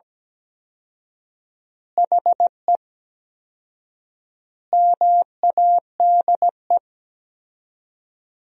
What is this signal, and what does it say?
Morse code, heard as a keyed tone, about 17 wpm, 700 Hz. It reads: HE MADE
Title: 4x Word-Spacing - Sentences from Top 300 Words - 17wpm